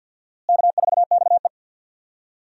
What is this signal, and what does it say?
Morse code, heard as a keyed tone, about 51 wpm, 700 Hz. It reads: K4XE
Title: USA Call Signs - 4 Characters - Repeated 2x - 50wpm